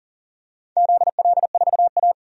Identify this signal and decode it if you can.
Morse code, heard as a keyed tone, about 37 wpm, 700 Hz. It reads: ZL4A
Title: DXCC-100 Call Signs - 4 Characters - 40wpm